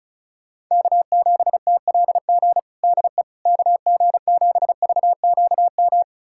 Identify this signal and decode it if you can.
Morse code, heard as a keyed tone, about 36 wpm, 700 Hz. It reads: K7TLG DE KG7VQM
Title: USA Callsign DE Callsign 35wpm